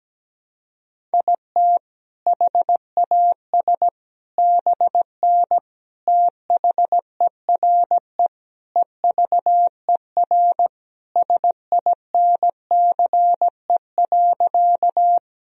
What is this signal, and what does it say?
Morse code, heard as a keyed tone, about 17 wpm, 700 Hz. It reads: IT HAS BN THERE EVER SINCE.
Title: Sentences from Top 400 Words and Abbreviations 17wpm